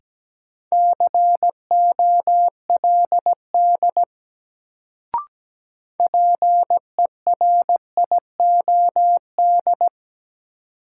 Morse code, heard as a keyed tone, about 17 wpm, 700 Hz. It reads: COLD PERIOD
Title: Sets of 5 Words - New Word Drill - 17wpm